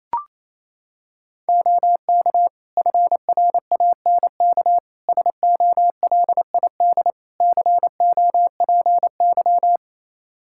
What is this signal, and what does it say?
Morse code, heard as a keyed tone, about 28 wpm, 700 Hz. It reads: OK FRANK SOLID COPY
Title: Practice QSO - Exchange 3 - More Info - 28wpm